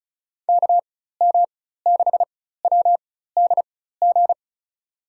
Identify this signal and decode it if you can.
Morse code, heard as a keyed tone, about 34 wpm, spaced as for 16 wpm, 700 Hz. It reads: KM6WDG
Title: USA Call Signs - 6 Characters - 4x Character Spacing - 35wpm